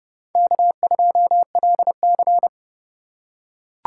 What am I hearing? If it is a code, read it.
Morse code, heard as a keyed tone, about 30 wpm, 700 Hz. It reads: K2LC
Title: USA Call Signs - 4 Characters - 3x Character Spacing - 30wpm